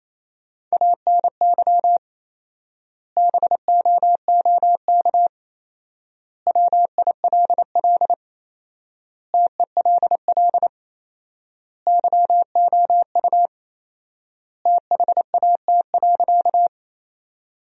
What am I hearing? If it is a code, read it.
Morse code, heard as a keyed tone, about 28 wpm, 700 Hz. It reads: ANY BOOK WILL TELL YOU THAT.
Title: 4x Word-Spacing - Sentences from Top 300 Words - 28wpm